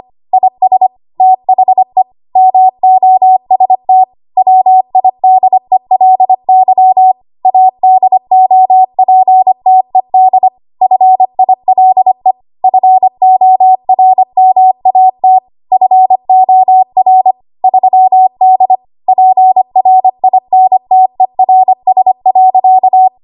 Morse code, heard as a keyed tone, about 25 wpm, 750 Hz. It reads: IS THE MOST WIDELY ADOPTED FILE FORMAT FOR 3D PRINTERS.
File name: Archive / 25 WPM / 25WPM